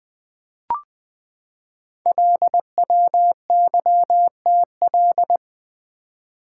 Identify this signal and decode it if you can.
Morse code, heard as a keyed tone, about 20 wpm, 700 Hz. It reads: LWYTL
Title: Code Groups - 5 Letters - 20wpm